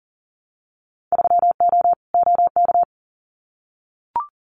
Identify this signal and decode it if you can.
Morse code, heard as a keyed tone, about 39 wpm, 700 Hz. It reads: 3O OK
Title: Winter Field Day Exchanges Encore 40wpm